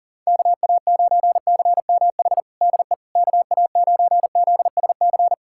Morse code, heard as a keyed tone, about 40 wpm, 700 Hz. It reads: KA9CMH DE KA9ZSC
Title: USA Callsign DE Callsign 40wpm